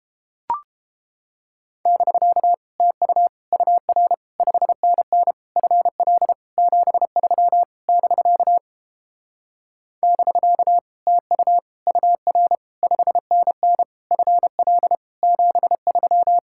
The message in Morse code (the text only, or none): <BK> TU UR 5NN FL 73 <BK> <BK> TU UR 5NN FL 73